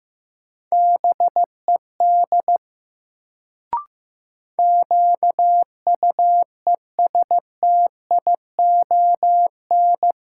Morse code, heard as a keyed tone, about 16 wpm, 700 Hz. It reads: BED QUESTION
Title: Sets of 5 Words Encore - New Word Drill - 15wpm